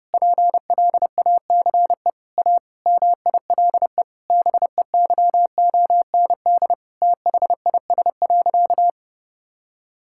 Morse code, heard as a keyed tone, about 31 wpm, 700 Hz. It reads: PLACE A MILE BEYOND THIS.
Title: Sentences from Top 1000 Words - Review Part 6 - 30wpm